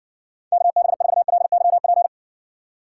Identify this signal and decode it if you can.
Morse code, heard as a keyed tone, about 60 wpm, 700 Hz. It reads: KB4LYP